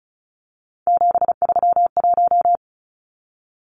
Morse code, heard as a keyed tone, about 34 wpm, 700 Hz. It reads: 731
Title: Numbers - 3-Digit 35wpm